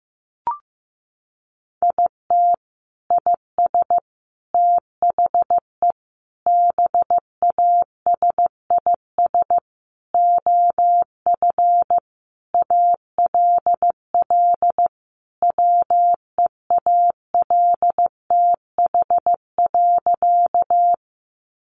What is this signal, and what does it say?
Morse code, heard as a keyed tone, about 15 wpm, 700 Hz. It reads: IT IS THE BASIS OF ALL WEALTH.
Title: Sentences from Top 1200 Words 15wpm